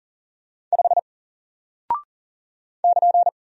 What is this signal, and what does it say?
Morse code, heard as a keyed tone, about 38 wpm, 700 Hz. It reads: 5 <KN>
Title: Single Character (Letter:Number:Punctuation:Prosign) Alt 40wpm